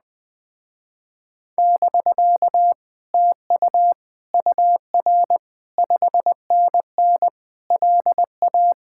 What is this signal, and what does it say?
Morse code, heard as a keyed tone, about 20 wpm, 700 Hz. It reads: <BK> TU UR 5NN LA
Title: Speed-Racing Common QSO Phrases - 17wpm